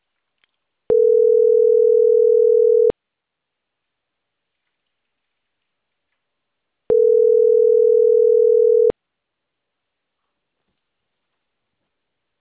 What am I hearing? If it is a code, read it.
Ringback tone (North America), 440 + 480 Hz.